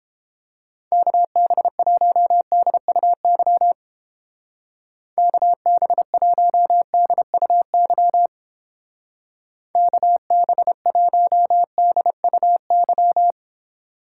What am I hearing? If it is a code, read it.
Morse code, heard as a keyed tone, about 28 wpm, 700 Hz. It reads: KB1DUY KB1DUY KB1DUY